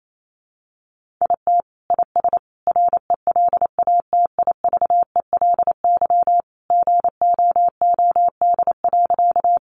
Morse code, heard as a keyed tone, about 28 wpm, 700 Hz. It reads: IT IS RELATIVELY GOOD.
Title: Sentences from Top 1100 Words 28wpm